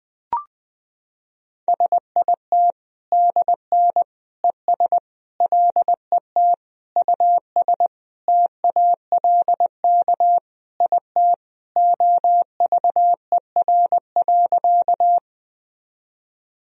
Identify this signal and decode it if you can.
Morse code, heard as a keyed tone, about 20 wpm, 700 Hz. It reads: SIT DN ES LET US TALK IT OVER.
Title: Sentences from Top 500 Words and Abbreviations 20wpm